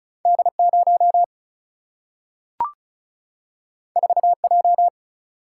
Morse code, heard as a keyed tone, about 36 wpm, 700 Hz. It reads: D0 4J
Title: Single-Letter Single-Number 35wpm